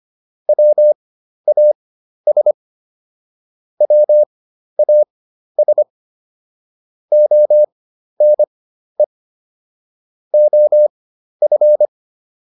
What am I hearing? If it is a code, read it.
Morse code, heard as a keyed tone, about 25 wpm, spaced as for 12 wpm, 600 Hz. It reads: WAS WAS ONE OF